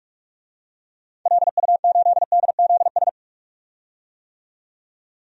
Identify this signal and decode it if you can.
Morse code, heard as a keyed tone, about 41 wpm, 700 Hz. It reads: LU8DZS
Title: DXCC-100 Call Signs - 6 Characters - 45wpm